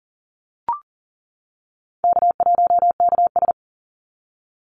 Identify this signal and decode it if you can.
Morse code, heard as a keyed tone, about 40 wpm, 700 Hz. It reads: K1KS
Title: Easy Call Signs - 40wpm